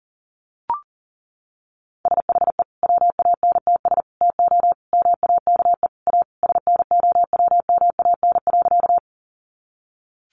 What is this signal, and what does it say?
Morse code, heard as a keyed tone, about 40 wpm, 700 Hz. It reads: SHE WANTS TO MAKE A SNOWMAN.
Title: Easy Phrases 40wpm